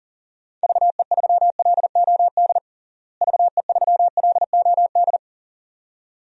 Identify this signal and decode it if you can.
Morse code, heard as a keyed tone, about 37 wpm, 700 Hz. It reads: VE3LOD VE3LOD
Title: DXCC-100 Call Signs - 6 Characters - Repeated 2x - 40wpm